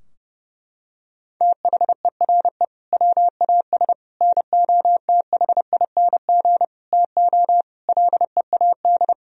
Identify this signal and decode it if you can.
Morse code, heard as a keyed tone, about 32 wpm, 700 Hz. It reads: THERE WAS NOTHING TO LEAD
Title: Sentences from Top 1000 Words - Review Part 2 - 30wpm